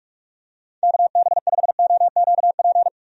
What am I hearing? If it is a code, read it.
Morse code, heard as a keyed tone, about 43 wpm, 700 Hz. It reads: KB5OQP
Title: USA Call Signs - 6 Characters - 45wpm